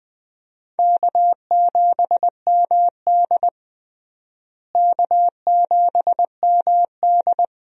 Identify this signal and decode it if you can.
Morse code, heard as a keyed tone, about 21 wpm, 700 Hz. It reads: K7MD K7MD
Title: USA Call Signs - 4 Characters - Repeated 2x - 20wpm